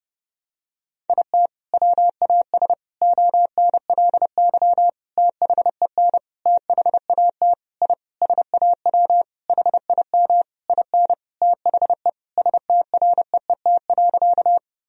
Morse code, heard as a keyed tone, about 30 wpm, 700 Hz. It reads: IT WAS ONLY THEN THAT I SAW HIM IN THE STREET.
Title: Sentences from Top 1000 Words - Review Part 3 - 30wpm